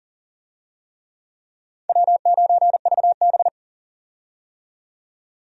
Morse code, heard as a keyed tone, about 40 wpm, 700 Hz. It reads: W9VB
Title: USA Call Signs - 4 Characters - 40wpm